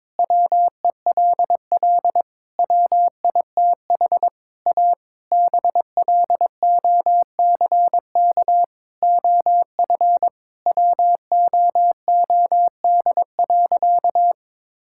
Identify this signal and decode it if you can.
Morse code, heard as a keyed tone, about 21 wpm, 700 Hz. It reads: WELL WITH A BLOCK OF WOOD.
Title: Sentences from Top 1600 Words 22wpm